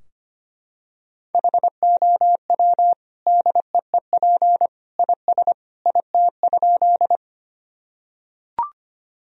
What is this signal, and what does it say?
Morse code, heard as a keyed tone, about 25 wpm, 700 Hz. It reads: HOW DEEP IS IT?
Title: Sentences from Top 1000 Words - Review Part 3 - 25wpm